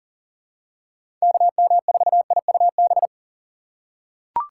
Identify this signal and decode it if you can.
Morse code, heard as a keyed tone, about 40 wpm, 700 Hz. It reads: KM4IUB
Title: USA Call Signs - 6 Characters - 2x Character Spacing - 40wpm